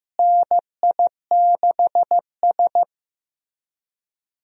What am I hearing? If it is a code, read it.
Morse code, heard as a keyed tone, about 15 wpm, 700 Hz. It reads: NI6S
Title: USA Call Signs - 4 Characters - Repeated 3x - 15wpm